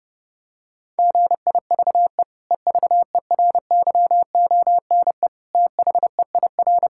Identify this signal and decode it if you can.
Morse code, heard as a keyed tone, about 32 wpm, 700 Hz. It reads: GIVE EVERYONE THEIR
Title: Sentences from Top 1000 Words - Review Part 6 - 30wpm